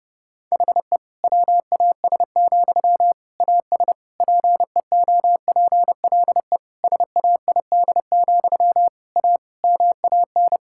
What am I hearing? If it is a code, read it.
Morse code, heard as a keyed tone, about 30 wpm, 700 Hz. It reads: HE WAS, AS PEOPLE SAID, A MAN
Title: Sentences from Top 1000 Words - Review Part 6 - 30wpm